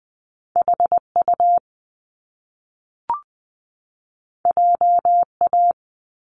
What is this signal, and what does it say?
Morse code, heard as a keyed tone, about 20 wpm, 700 Hz. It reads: HU JA
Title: Top Two Letter Combinations in Words 20wpm